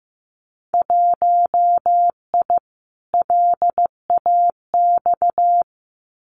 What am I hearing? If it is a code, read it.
Morse code, heard as a keyed tone, about 15 wpm, 700 Hz. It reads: 1I LAX
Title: Winter Field Day Exchanges Encore 15wpm